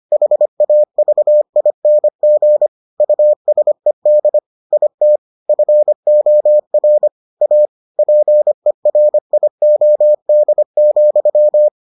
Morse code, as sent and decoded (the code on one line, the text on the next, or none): .... .- ...- .. -. --. / ..- ... -.. / .. - / ..-. --- .-. / .- / .--. .-. .. --- -.. --..--
HAVING USED IT FOR A PERIOD,